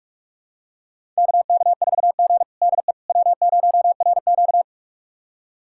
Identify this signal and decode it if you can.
Morse code, heard as a keyed tone, about 46 wpm, 700 Hz. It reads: KK4G DE W0RQ
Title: USA Callsign DE Callsign - 4 Characters - 45wpm